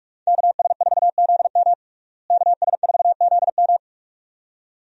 Morse code, heard as a keyed tone, about 46 wpm, 700 Hz. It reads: KS4ZM KS4ZM
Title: USA Call Signs - 5 Characters - Repeated 2x - 45wpm